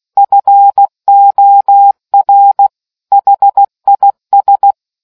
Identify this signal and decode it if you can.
Morse code, heard as a keyed tone, about 16 wpm, 800 Hz. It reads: FOR HIS